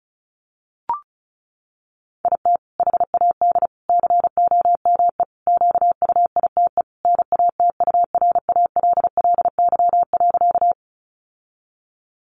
Morse code, heard as a keyed tone, about 34 wpm, 700 Hz. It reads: IT HAD COME QUITE NATURALLY.
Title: Sentences from Top 1300 Words 35wpm